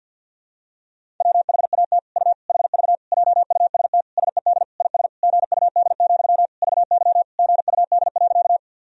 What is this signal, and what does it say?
Morse code, heard as a keyed tone, about 52 wpm, 700 Hz. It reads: WHAT U HV JUST SED IS GUD, VY GUD.
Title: Sentences from Top 200 Words and Abbreviations 50wpm